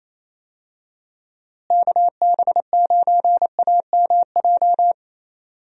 Morse code, heard as a keyed tone, about 28 wpm, 700 Hz. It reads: KB9AMJ
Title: USA Call Signs - 6 Characters - Repeated 3x - 28wpm